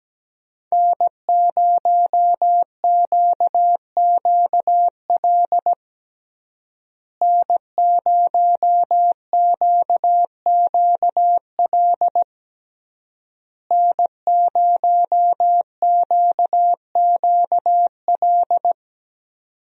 Morse code, heard as a keyed tone, about 17 wpm, 700 Hz. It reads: N0QQL N0QQL N0QQL
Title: USA Call Signs - 5 Characters - Repeated 3x - 17wpm